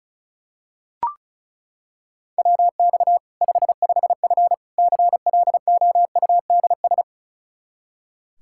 Morse code, heard as a keyed tone, about 34 wpm, 700 Hz. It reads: WX 55F CLOUDS
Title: Practice QSO - Exchange 2 - Info - 35wpm